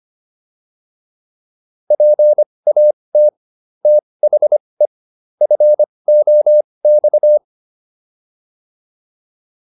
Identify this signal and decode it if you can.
Morse code, heard as a keyed tone, about 25 wpm, spaced as for 20 wpm, 600 Hz. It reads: PAT THE FOX